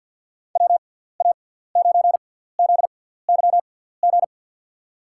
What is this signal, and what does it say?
Morse code, heard as a keyed tone, about 53 wpm, spaced as for 17 wpm, 700 Hz. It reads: WA9ZYG
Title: USA Call Signs - 6 Characters - 6x Character Spacing - 50wpm